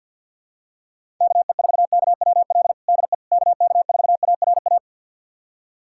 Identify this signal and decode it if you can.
Morse code, heard as a keyed tone, about 50 wpm, 700 Hz. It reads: KE4KWL DE KK4ARA
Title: USA Callsign DE Callsign - 6 Characters - 50wpm